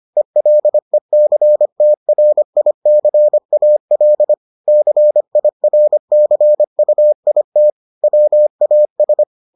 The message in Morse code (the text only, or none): ELECTRICAL CIRCUIT WAS